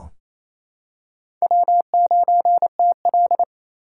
Morse code, heard as a keyed tone, about 28 wpm, 700 Hz. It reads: W9TL
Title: USA Call Signs - 4 Characters - 2x Character Spacing - 28wpm